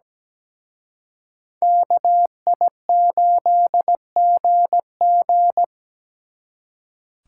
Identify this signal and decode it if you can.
Morse code, heard as a keyed tone, about 20 wpm, 700 Hz. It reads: KI8GG
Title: Speed Racing - USA 2x2 Call Signs 17wpm